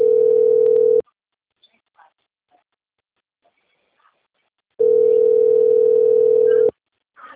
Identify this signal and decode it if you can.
Ringback tone (North America), 440 + 480 Hz.
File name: out- - - - - .86712